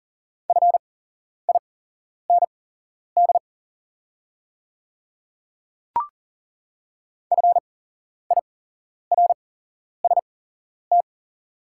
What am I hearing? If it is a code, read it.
Morse code, heard as a keyed tone, about 40 wpm, spaced as for 11 wpm, 700 Hz. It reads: FIND FIRST
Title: 8x Character Spacing - Top 100 Words - 40wpm